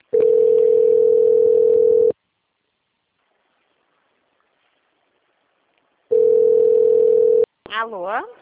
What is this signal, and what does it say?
Ringback tone (North America), 440 + 480 Hz, with speech.
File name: out- - - - - .1966